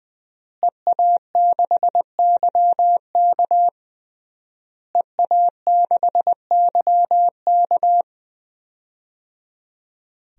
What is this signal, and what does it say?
Morse code, heard as a keyed tone, about 21 wpm, 700 Hz. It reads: EA6YK EA6YK
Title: DXCC-100 Call Signs - 5 Characters - Repeated 3x - 20wpm